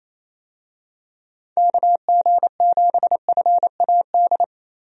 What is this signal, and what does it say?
Morse code, heard as a keyed tone, about 28 wpm, 700 Hz. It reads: KG7FAD
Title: USA Call Signs - 6 Characters - Repeated 2x - 28wpm